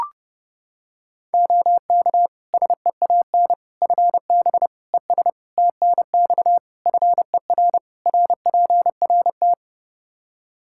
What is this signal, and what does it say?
Morse code, heard as a keyed tone, about 30 wpm, 700 Hz. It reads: OK SEAN FB ES TNX FER RPRT
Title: Practice QSO - Exchange 2 - Info - 30wpm